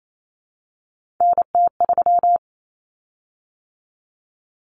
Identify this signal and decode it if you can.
Morse code, heard as a keyed tone, about 28 wpm, 700 Hz. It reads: NT3
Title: Common Cut Numbers - 3-Digit 28wpm